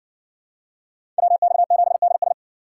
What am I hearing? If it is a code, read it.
Morse code, heard as a keyed tone, about 58 wpm, 700 Hz. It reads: WX7DS